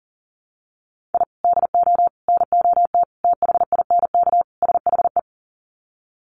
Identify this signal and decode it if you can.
Morse code, heard as a keyed tone, about 40 wpm, 700 Hz. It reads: I DO NOT THINK SHE
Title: Sets of 5 Words 40wpm